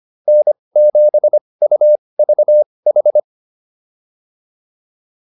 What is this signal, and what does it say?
Morse code, heard as a keyed tone, about 25 wpm, spaced as for 20 wpm, 600 Hz. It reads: N7UVH
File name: QSO105a 20